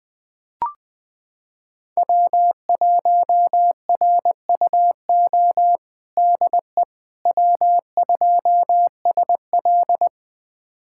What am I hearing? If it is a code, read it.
Morse code, heard as a keyed tone, about 20 wpm, 700 Hz. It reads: W1RUO DE W2SL
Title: USA Callsign DE Callsign 20wpm